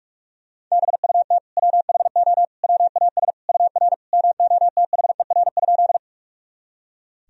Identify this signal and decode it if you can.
Morse code, heard as a keyed tone, about 41 wpm, 700 Hz. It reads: BUT WHO WAS UR MOTHER?
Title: Sentences from Top 200 Words and Abbreviations 45wpm